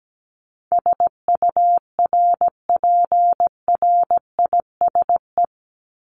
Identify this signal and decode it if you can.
Morse code, heard as a keyed tone, about 17 wpm, 700 Hz. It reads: SURPRISE
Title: Top 1751-2000 Words 17wpm